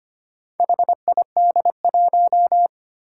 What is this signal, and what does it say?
Morse code, heard as a keyed tone, about 25 wpm, 700 Hz. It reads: HID1
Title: Four Challenging Letter-Number Combinations 25wpm